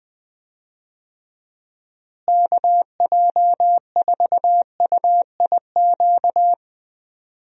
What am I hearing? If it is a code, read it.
Morse code, heard as a keyed tone, about 20 wpm, 700 Hz. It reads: KJ4UIQ
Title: USA Call Signs - 6 Characters - Repeated 2x - 20wpm